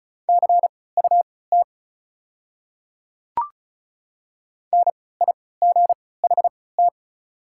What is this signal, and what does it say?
Morse code, heard as a keyed tone, about 34 wpm, spaced as for 20 wpm, 700 Hz. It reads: CUT NIGHT
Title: 3x Character Spacing - Top 500 Words - 35wpm